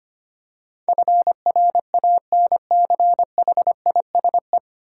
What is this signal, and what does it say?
Morse code, heard as a keyed tone, about 25 wpm, 700 Hz. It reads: FRANCHISE
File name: 9 Letter Words - First 3 Letters Given - 25wpm